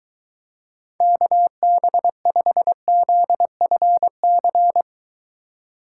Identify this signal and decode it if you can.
Morse code, heard as a keyed tone, about 23 wpm, 700 Hz. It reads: KB5ZFC